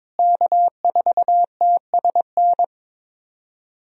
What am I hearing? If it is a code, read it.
Morse code, heard as a keyed tone, about 22 wpm, 700 Hz. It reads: K4TSN
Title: USA Call Signs - 5 Characters - Repeated 2x - 22wpm